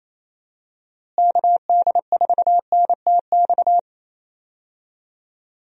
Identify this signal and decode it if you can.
Morse code, heard as a keyed tone, about 27 wpm, 700 Hz. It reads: KD4NTX
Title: USA Call Signs - 6 Characters - Repeated 2x - 28wpm